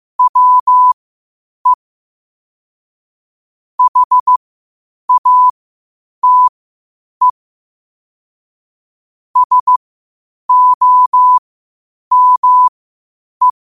Morse code, heard as a keyed tone, about 15 wpm, spaced as for 8 wpm, 1000 Hz. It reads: WE HATE SOME
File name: qotdfiles / 8wpm